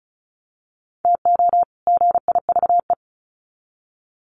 Morse code, heard as a keyed tone, about 34 wpm, 700 Hz. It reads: TO GIVE
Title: Sets of 2 Words 35wpm